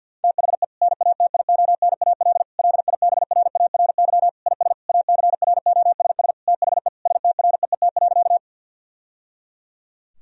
Morse code, heard as a keyed tone, about 50 wpm, 700 Hz. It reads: THE NATIONAL LIBRARY IS ACROSS THE STREET.
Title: Sentences from Top 1000 Words - Review Part 4 - 50wpm